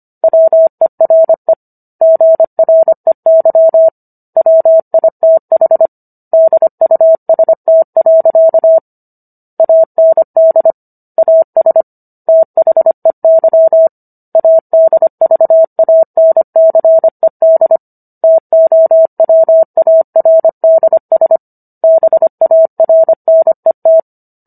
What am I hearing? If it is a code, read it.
Morse code, heard as a keyed tone, about 25 wpm, 650 Hz. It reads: WERE GREY WITH DUST. AND AS THEY ADVANCED TOWARDS BARNET
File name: Book One / 16-Chapter 16 - 25 WPM